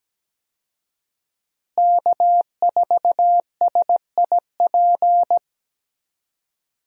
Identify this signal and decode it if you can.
Morse code, heard as a keyed tone, about 17 wpm, 700 Hz. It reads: K4SIP
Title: USA Call Signs - 5 Characters - Repeated 2x - 17wpm